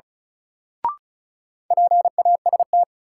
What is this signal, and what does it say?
Morse code, heard as a keyed tone, about 36 wpm, 700 Hz. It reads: PAST
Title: Sentences from Top 600 Words - New Word Drill - 35wpm